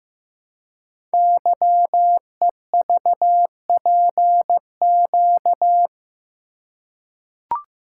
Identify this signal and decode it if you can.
Morse code, heard as a keyed tone, about 15 wpm, 700 Hz. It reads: YEVPQ
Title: Code Groups - 5 Letters - 15wpm